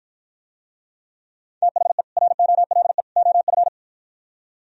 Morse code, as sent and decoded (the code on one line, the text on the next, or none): - .... / .-. --- .-.. / --- ..-.
THE ROLE OF